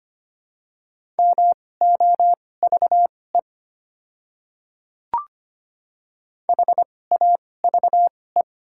Morse code, heard as a keyed tone, about 25 wpm, spaced as for 18 wpm, 690 Hz. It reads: MOVE HAVE
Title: 2x Character Spacing - Top 300 Words - 25wpm